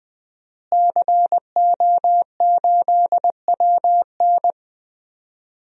Morse code, heard as a keyed tone, about 20 wpm, 700 Hz. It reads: CO8WN
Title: DXCC-100 Call Signs - 5 Characters - Repeated 2x - 20wpm